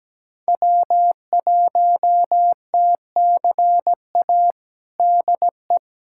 Morse code, heard as a keyed tone, about 17 wpm, 700 Hz. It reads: W1TCA DE
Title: USA Callsign DE Callsign 17wpm